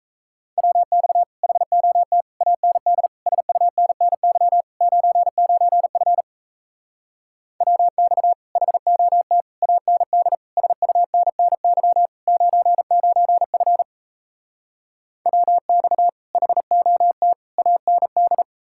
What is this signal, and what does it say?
Morse code, heard as a keyed tone, about 37 wpm, 700 Hz. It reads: WX HOT AND SUNNY 99F WX HOT AND SUNNY 99F WX HOT AND